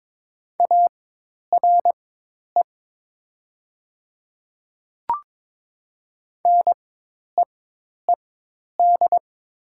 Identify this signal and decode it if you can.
Morse code, heard as a keyed tone, about 23 wpm, spaced as for 10 wpm, 700 Hz. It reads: ARE NEED
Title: 4x Character Spacing - Top 200 Words - 22wpm